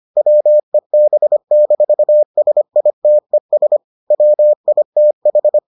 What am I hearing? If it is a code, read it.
Morse code, heard as a keyed tone, about 25 wpm, 600 Hz. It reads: WEB-SITES WITH